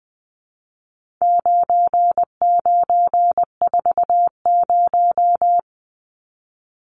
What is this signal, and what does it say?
Morse code, heard as a keyed tone, about 20 wpm, 700 Hz. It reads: 9940
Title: Numbers - 4-Digit 20wpm